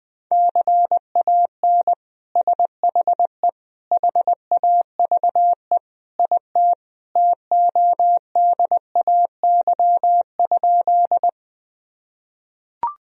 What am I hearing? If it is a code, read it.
Morse code, heard as a keyed tone, about 21 wpm, 700 Hz. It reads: CAN SHE HAVE IT TODAY?
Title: Sentences from Top 1000 Words - Review Part 4 - 20wpm